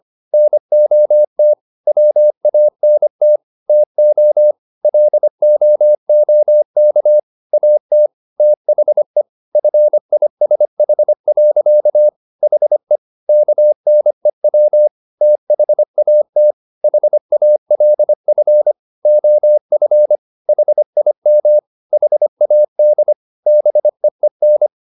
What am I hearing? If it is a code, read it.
Morse code, heard as a keyed tone, about 25 wpm, 600 Hz. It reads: NOT WANT TO LOOK AT THE FISH. HE KNEW THAT HALF OF HIM HAD BEEN